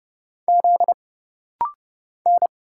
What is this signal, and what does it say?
Morse code, heard as a keyed tone, about 30 wpm, 700 Hz. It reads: Z N
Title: Single Letter-Number Alt - Rapid-Fire 30wpm